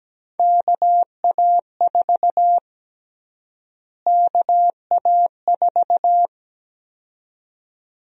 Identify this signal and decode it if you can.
Morse code, heard as a keyed tone, about 17 wpm, 700 Hz. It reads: KA4 KA4
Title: USA Call Sign Prefixes - Repeated 2x - 17wpm